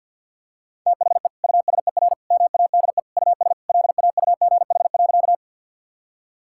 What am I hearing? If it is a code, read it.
Morse code, heard as a keyed tone, about 52 wpm, 700 Hz. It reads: THE USER MADE US LAUGH.
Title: Sentences from Top 1000 Words - Review Part 5 - 50wpm